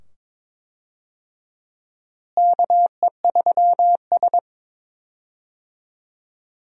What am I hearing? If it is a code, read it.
Morse code, heard as a keyed tone, about 22 wpm, 700 Hz. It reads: KE3S
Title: USA Call Signs - 4 Characters - Repeated 2x - 22wpm